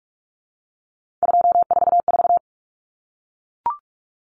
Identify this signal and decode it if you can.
Morse code, heard as a keyed tone, about 44 wpm, 700 Hz. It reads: 244
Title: Numbers - 3-Digit 45wpm